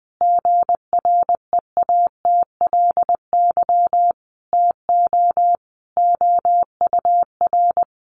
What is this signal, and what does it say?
Morse code, heard as a keyed tone, about 20 wpm, 700 Hz. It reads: GREATLY TO OUR
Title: Sentences from Top 1400 Words 20wpm